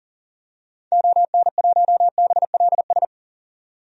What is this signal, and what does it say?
Morse code, heard as a keyed tone, about 40 wpm, 700 Hz. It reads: ON1BLS
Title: DXCC-100 Call Signs - 6 Characters - 40wpm